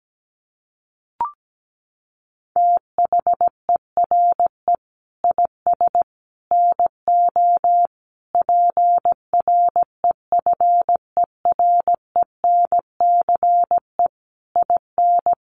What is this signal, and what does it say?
Morse code, heard as a keyed tone, about 17 wpm, 700 Hz. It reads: THERE IS NO PREFERENCE IN
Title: Sentences from Top 1500 Words 17wpm